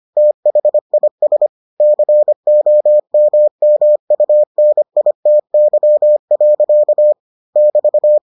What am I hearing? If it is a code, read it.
Morse code, heard as a keyed tone, about 25 wpm, 600 Hz. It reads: THIS COMMUNITY. <BT>